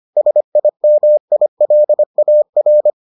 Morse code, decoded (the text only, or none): SIMILAR